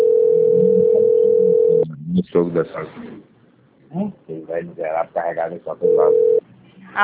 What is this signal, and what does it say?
Ringback tone (North America), 440 + 480 Hz, with speech.
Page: Bebedouro - SP